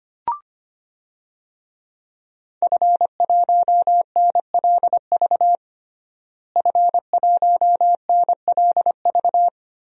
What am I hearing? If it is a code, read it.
Morse code, heard as a keyed tone, about 25 wpm, 700 Hz. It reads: F1NLV F1NLV
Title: DXCC-100 Call Signs - 5 Characters - Repeated 3x - 25wpm